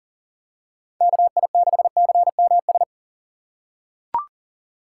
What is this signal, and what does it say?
Morse code, heard as a keyed tone, about 39 wpm, 700 Hz. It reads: KI6CMS
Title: USA Call Signs - 6 Characters - 4x Character Spacing - 40wpm